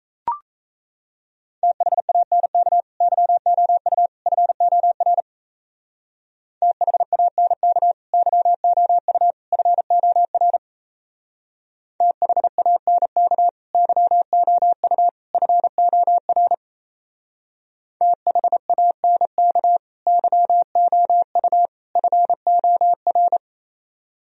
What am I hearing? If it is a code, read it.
Morse code, heard as a keyed tone, about 32 wpm, 700 Hz. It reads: THANK YOU FOR THANK YOU FOR THANK YOU FOR THANK YOU FOR
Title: Speed-Racing - Sets of 3 Words 28wpm